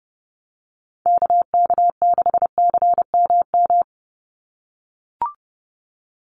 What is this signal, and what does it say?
Morse code, heard as a keyed tone, about 30 wpm, 700 Hz. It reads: KK6CMM
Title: USA Call Signs 30wpm